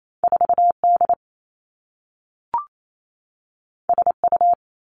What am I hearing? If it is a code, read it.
Morse code, heard as a keyed tone, about 28 wpm, 700 Hz. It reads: 4 D S U
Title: Two Challenging Letter-Number Combinations 28wpm